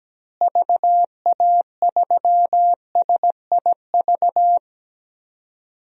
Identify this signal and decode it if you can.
Morse code, heard as a keyed tone, about 17 wpm, 700 Hz. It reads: VA3SIV
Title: DXCC-100 Call Signs - 6 Characters - Repeated 2x - 17wpm